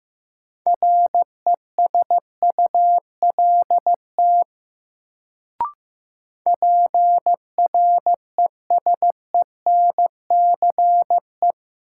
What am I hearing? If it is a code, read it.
Morse code, heard as a keyed tone, about 15 wpm, 700 Hz. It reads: RESULT PRESENCE
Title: Sets of 3 Words - New Word Drill - 15wpm